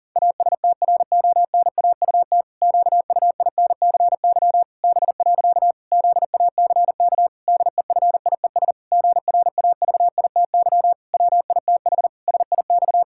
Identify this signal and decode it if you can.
Morse code, heard as a keyed tone, about 40 wpm, 700 Hz. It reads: ASTRONAUT QUINCY B. ZACK DEFIES GRAVITY WITH SIX
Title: Pangram Challenge v2 - 40wpm